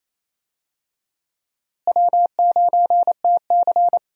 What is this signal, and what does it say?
Morse code, heard as a keyed tone, about 28 wpm, 700 Hz. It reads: W9TC
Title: USA Call Signs - 4 Characters - Repeated 2x - 28wpm